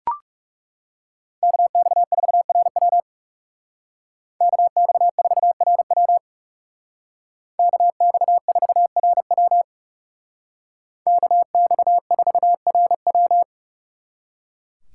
Morse code, heard as a keyed tone, about 37 wpm, 700 Hz. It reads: KX4RW KX4RW KX4RW KX4RW
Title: Speed Racing - USA 2x2 Call Signs 30wpm